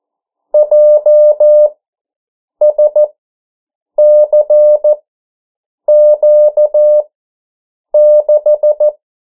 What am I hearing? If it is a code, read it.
Morse code, heard as a keyed tone, about 14 wpm, spaced as for 7 wpm, 600 Hz. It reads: JSCQ6
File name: morse mp3 / 09 urok 8